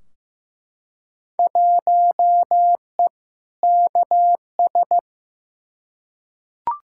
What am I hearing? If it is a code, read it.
Morse code, heard as a keyed tone, about 15 wpm, 700 Hz. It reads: 1E KS
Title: Field Day Exchanges - 15wpm